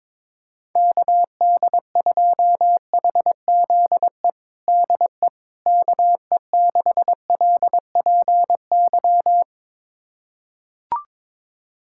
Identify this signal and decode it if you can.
Morse code, heard as a keyed tone, about 22 wpm, 700 Hz. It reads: KD2HZE DE KE6LPY
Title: USA Callsign DE Callsign - 6 Characters - 22wpm